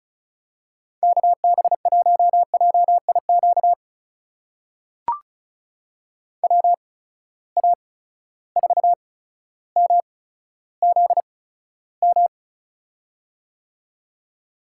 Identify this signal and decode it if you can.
Morse code, heard as a keyed tone, about 35 wpm, spaced as for 15 wpm, 700 Hz. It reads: KB1JIQ WA4MZM
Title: USA Call Signs - 6 Characters - 8x Character Spacing - 35wpm